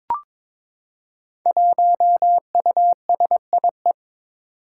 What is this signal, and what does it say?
Morse code, heard as a keyed tone, about 22 wpm, 700 Hz. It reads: 1USIE
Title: Five Challenging Letter-Number Combinations 22wpm